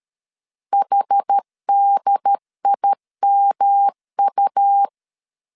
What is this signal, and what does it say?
Morse code, heard as a keyed tone, about 13 wpm, 800 Hz. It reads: HDIMU
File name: CW / DET / 1 3 1500C S B 12wpm 2828